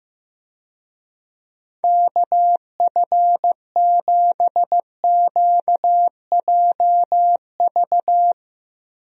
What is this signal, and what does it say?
Morse code, heard as a keyed tone, about 15 wpm, 700 Hz. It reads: KF7QJV
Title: USA Call Signs - 6 Characters - Repeated 3x - 15wpm